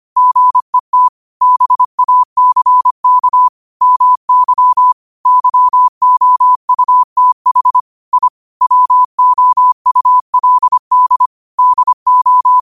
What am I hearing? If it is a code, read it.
Morse code, heard as a keyed tone, about 25 wpm, 1000 Hz. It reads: GET BACK MY YOUTH I WOULD DO
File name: qotdfiles / 25wpm